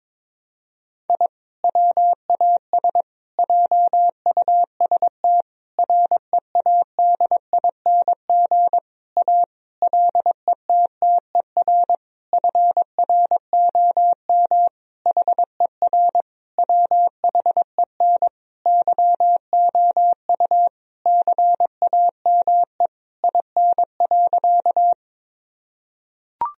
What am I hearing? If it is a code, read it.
Morse code, heard as a keyed tone, about 22 wpm, 690 Hz. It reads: I WAS JUST READING A LETTER FROM HER WHEN YOU CAME IN.
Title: Sentences from Top 1000 Words - Review Part 3 - 22wpm